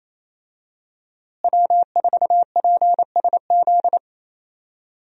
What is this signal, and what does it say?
Morse code, heard as a keyed tone, about 28 wpm, 700 Hz. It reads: W4PSZ
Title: USA Call Signs - 5 Characters - Repeated 3x - 28wpm